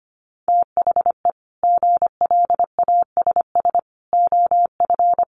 Morse code, heard as a keyed tone, about 25 wpm, 700 Hz. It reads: THE GLASS OF
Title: Additional Easy Phrases 25wpm